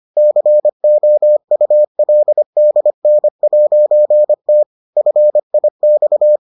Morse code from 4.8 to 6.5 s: FIX

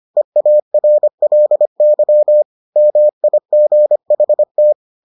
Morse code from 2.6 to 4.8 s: MIGHT

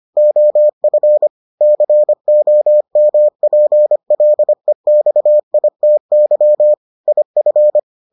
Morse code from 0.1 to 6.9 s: OF COMPLEXITY

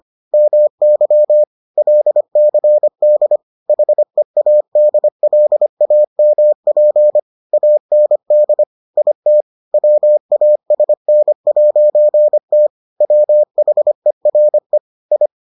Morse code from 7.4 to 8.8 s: AND

